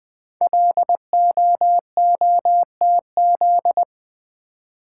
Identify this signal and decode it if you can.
Morse code, heard as a keyed tone, about 20 wpm, 700 Hz. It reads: LOOTZ
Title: Code Groups - 5 Letters - 20wpm